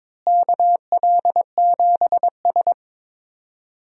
Morse code, heard as a keyed tone, about 22 wpm, 700 Hz. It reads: KL7S
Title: DXCC-100 Call Signs - 4 Characters - Repeated 2x - 22wpm